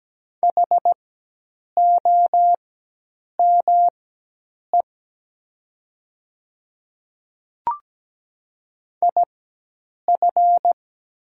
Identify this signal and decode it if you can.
Morse code, heard as a keyed tone, about 17 wpm, spaced as for 8 wpm, 700 Hz. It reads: HOME IF
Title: 4x Character Spacing - Top 200 Words - 17wpm